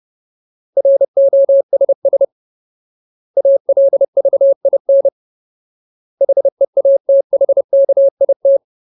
Morse code, heard as a keyed tone, about 30 wpm, 550 Hz. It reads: ROSS ALVIN HEATHKIT
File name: Ultimate Word List / 30WPM Ultimate